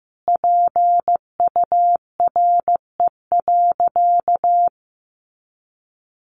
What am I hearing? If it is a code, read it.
Morse code, heard as a keyed tone, about 14 wpm, 700 Hz. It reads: PURE.
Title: Sentences from Top 1500 Words 15wpm